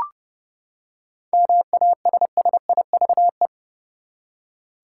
Morse code, heard as a keyed tone, about 30 wpm, 700 Hz. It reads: MASSIVE
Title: Seven Letter Words - 30wpm